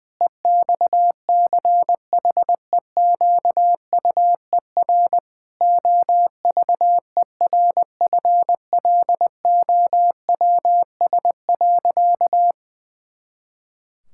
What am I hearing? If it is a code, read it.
Morse code, heard as a keyed tone, about 21 wpm, 700 Hz. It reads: EXCHEQUER OVERFLOWS.
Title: Pangram Challenge v2 - 20wpm